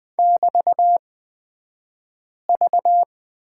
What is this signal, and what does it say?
Morse code, heard as a keyed tone, about 20 wpm, spaced as for 5 wpm, 700 Hz. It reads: <BT>V
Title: 223 Lesson 36d - BT - Two Characters 20wpm